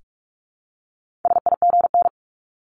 Morse code, heard as a keyed tone, about 46 wpm, 700 Hz. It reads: SIGN
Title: Multi-letter Phonemes 45wpm